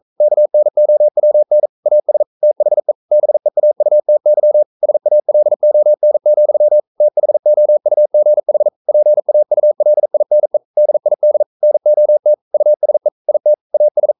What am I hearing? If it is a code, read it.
Morse code, heard as a keyed tone, about 42 wpm, 600 Hz. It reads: KNOWN AS THE BEAUTY SALON, THOUGH PAULINE DID NOT USE IT AS
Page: Romanen Never af Ken Follet på engelsk.